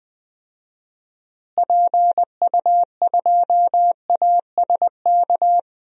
Morse code, heard as a keyed tone, about 20 wpm, 700 Hz. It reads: PU2ASK
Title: DXCC-100 Call Signs - 6 Characters - Repeated 3x - 20wpm